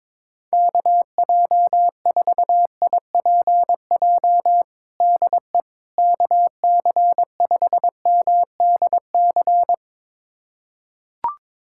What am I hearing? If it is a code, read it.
Morse code, heard as a keyed tone, about 21 wpm, 700 Hz. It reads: KJ4IPJ DE KC5MDC
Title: USA Callsign DE Callsign - 6 Characters - 22wpm